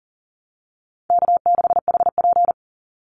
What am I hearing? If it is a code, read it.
Morse code, heard as a keyed tone, about 40 wpm, 700 Hz. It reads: K6HP
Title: Easy Call Signs - 40wpm